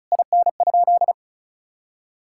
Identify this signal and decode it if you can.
Morse code, heard as a keyed tone, about 34 wpm, 700 Hz. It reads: IN?
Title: Sentences from Top 1000 Words - Review Part 5 - 35wpm